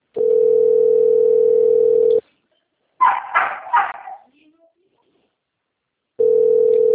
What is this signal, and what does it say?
Ringback tone (North America), 440 + 480 Hz, with speech.
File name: rg-1600-unknown- - - .48427